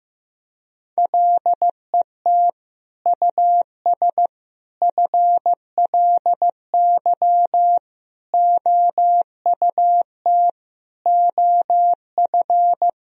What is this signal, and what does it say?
Morse code, heard as a keyed tone, about 15 wpm, 700 Hz. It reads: LET US FLY OUT OF
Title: Sentences from Top 1000 Words - Review Part 2 - 15wpm